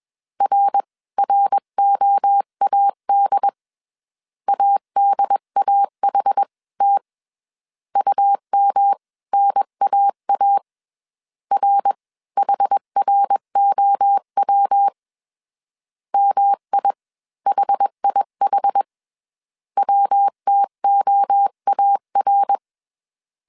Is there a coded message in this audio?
Yes: Morse code from 0.4 to 22.6 s, heard as a keyed tone, about 21 wpm, spaced as for 18 wpm, 800 Hz. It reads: RROAD ADAHT UMNAA RHROW MIHIH WTOAR